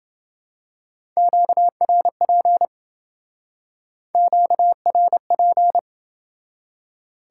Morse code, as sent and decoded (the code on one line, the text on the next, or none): --.- .-. .--. / --.- .-. .--.
QRP QRP